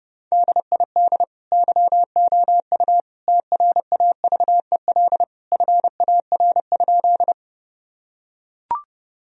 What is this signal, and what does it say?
Morse code, heard as a keyed tone, about 30 wpm, 700 Hz. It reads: DID YOU TRAVEL FAR?
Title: Sentences from Top 1000 Words - Review Part 2 - 30wpm